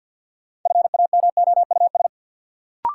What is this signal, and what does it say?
Morse code, heard as a keyed tone, about 53 wpm, 700 Hz. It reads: FAMOUS
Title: Sentences from Top 1500 Words - New Word Drill Alt - 50wpm